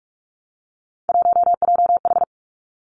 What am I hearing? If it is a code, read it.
Morse code, heard as a keyed tone, about 46 wpm, 700 Hz. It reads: 1JH
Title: Three Challenging Letter-Number Combinations 45wpm